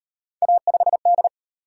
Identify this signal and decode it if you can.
Morse code, heard as a keyed tone, about 37 wpm, 700 Hz. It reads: A5D